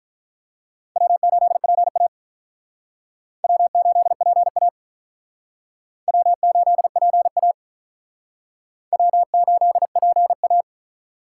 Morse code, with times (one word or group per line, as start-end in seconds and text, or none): 1.0-2.1 W8PA
3.4-4.7 W8PA
6.1-7.5 W8PA
8.9-10.6 W8PA